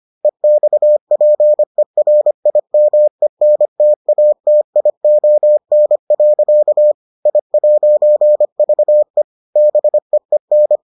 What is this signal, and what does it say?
Morse code, heard as a keyed tone, about 25 wpm, 600 Hz. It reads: EXPERIMENTATION. I'VE BEEN